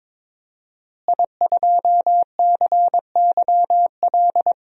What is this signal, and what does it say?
Morse code, heard as a keyed tone, about 22 wpm, 700 Hz. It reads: I2CYL
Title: DXCC-100 Call Signs - 5 Characters - Repeated 3x - 22wpm